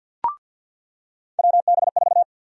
Morse code, heard as a keyed tone, about 48 wpm, 700 Hz. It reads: WB4
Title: USA Call Sign Prefixes - 50wpm